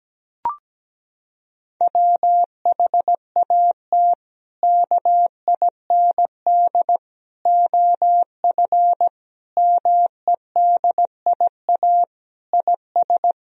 Morse code, heard as a keyed tone, about 17 wpm, 700 Hz. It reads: WHAT KIND OF MEDIA IS
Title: Sentences from Top 1000 Words - Review Part 4 - 17wpm